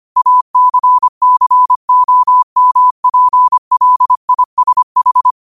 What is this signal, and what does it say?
Morse code, heard as a keyed tone, about 25 wpm, 1000 Hz. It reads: ACCOMPLISH
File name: qotdfiles / 25wpm